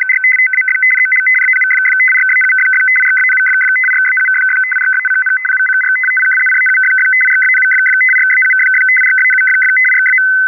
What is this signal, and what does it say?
Two-tone teleprinter signal, about 500 Hz shift.
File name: BUL-ITA5-CYR